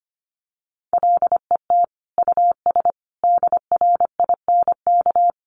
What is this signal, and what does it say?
Morse code, heard as a keyed tone, about 25 wpm, 700 Hz. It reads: LET US DRINK
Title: Sentences from Top 1500 Words 25wpm